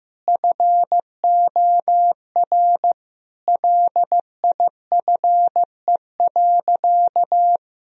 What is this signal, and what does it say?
Morse code, heard as a keyed tone, about 15 wpm, 700 Hz. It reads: FOR LIFE.
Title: Sentences from Top 1000 Words - Review Part 2 - 15wpm